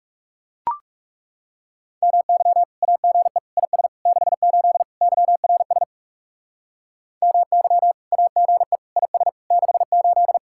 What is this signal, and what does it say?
Morse code, heard as a keyed tone, about 38 wpm, 700 Hz. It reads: MY AGE IS 68 YRS MY AGE IS 68